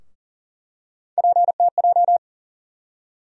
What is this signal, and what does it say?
Morse code, heard as a keyed tone, about 40 wpm, 700 Hz. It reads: PTJ
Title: USA Call Sign Postfixes - 3 Letters 40wpm